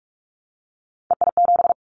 Morse code, heard as a keyed tone, about 46 wpm, 690 Hz. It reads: EI7
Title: Callsign Prefixes for Top 50 DXCC - 45wpm